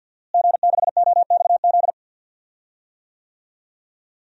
Morse code, heard as a keyed tone, about 52 wpm, 700 Hz. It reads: G6OXZ
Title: DXCC-100 Call Signs - 5 Characters - Repeated 3x - 50wpm